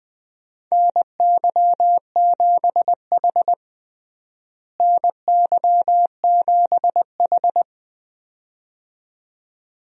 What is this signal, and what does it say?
Morse code, heard as a keyed tone, about 20 wpm, 700 Hz. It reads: NY7H NY7H
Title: USA Call Signs - 4 Characters - Repeated 2x - 20wpm